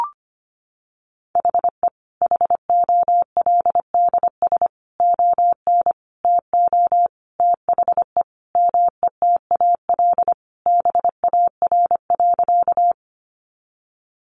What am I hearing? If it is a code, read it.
Morse code, heard as a keyed tone, about 25 wpm, 700 Hz. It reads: HE HOLDS ON TO THE METAL BAR.
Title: Additional Easy Phrases 25wpm